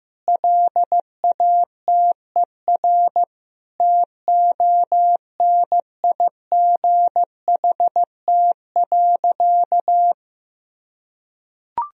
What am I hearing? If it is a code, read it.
Morse code, heard as a keyed tone, about 15 wpm, 700 Hz. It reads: LATER TONIGHT.
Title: Sentences from Top 1000 Words - Review Part 4 - 15wpm